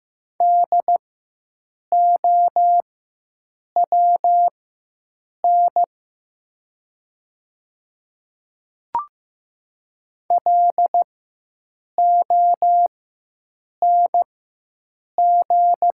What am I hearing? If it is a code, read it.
Morse code, heard as a keyed tone, about 15 wpm, spaced as for 7 wpm, 700 Hz. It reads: DOWN LONG
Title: 4x Character Spacing - Top 100 Words - 15wpm